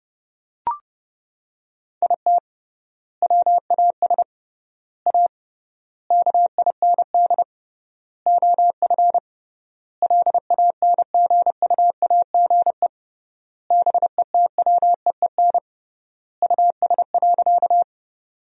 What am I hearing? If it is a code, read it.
Morse code, heard as a keyed tone, about 30 wpm, 700 Hz. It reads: IT WAS A KIND OF LANGUAGE BETWEEN US.
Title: 3x Word-Spacing - Sentences from Top 500 Words - 30wpm